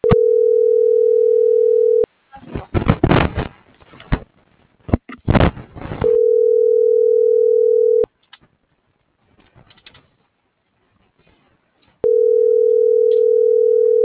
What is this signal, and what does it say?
Ringback tone (North America), 440 + 480 Hz.